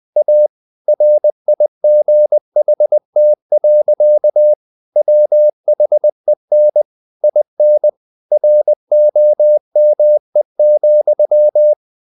Morse code, heard as a keyed tone, about 20 wpm, 600 Hz. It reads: A RIGHT. WHEN IN ROME,
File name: CW / Practice / EnglishSayings20